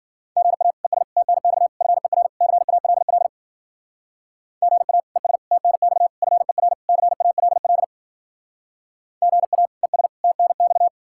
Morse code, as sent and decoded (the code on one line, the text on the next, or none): --. .- / . ... / - -. -..- / ..-. .-. / -.-. .- .-.. .-.. / --. .- / . ... / - -. -..- / ..-. .-. / -.-. .- .-.. .-.. / --. .- / . ... / - -. -..-
GA ES TNX FER CALL GA ES TNX FER CALL GA ES TNX